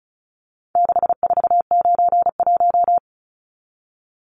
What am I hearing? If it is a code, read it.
Morse code, heard as a keyed tone, about 34 wpm, 700 Hz. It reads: 6491
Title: Numbers - 4-Digit 35wpm